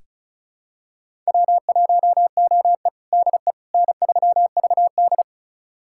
Morse code, heard as a keyed tone, about 36 wpm, 700 Hz. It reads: W1OE DE N3VD
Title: USA Callsign DE Callsign - 4 Characters - 35wpm